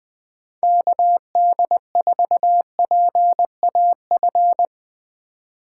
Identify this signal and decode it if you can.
Morse code, heard as a keyed tone, about 20 wpm, 700 Hz. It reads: KD4PAF